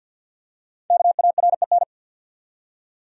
Morse code, heard as a keyed tone, about 50 wpm, 700 Hz. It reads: KAREN
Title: USA Common Names - Max 5 Letters - 50wpm